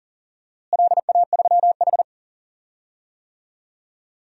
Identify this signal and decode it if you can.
Morse code, heard as a keyed tone, about 39 wpm, 700 Hz. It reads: LA3H
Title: DXCC-100 Call Signs - 4 Characters - Repeated 3x - 40wpm